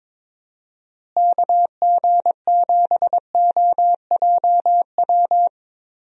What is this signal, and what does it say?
Morse code, heard as a keyed tone, about 22 wpm, 700 Hz. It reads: KG7OJW
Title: USA Call Signs - 6 Characters - Repeated 2x - 22wpm